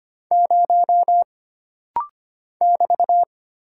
Morse code, heard as a keyed tone, about 25 wpm, 700 Hz. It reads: 0 <BT>
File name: Single Character (Letter:Number:Punctuation:Prosign) - Rapid-Fire - 25wpm